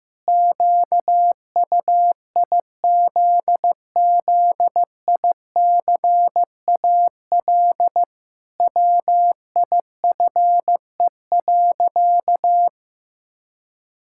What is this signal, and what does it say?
Morse code, heard as a keyed tone, about 15 wpm, 700 Hz. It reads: QUIZZICAL WIFE.
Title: Pangram Challenge v2 - 15wpm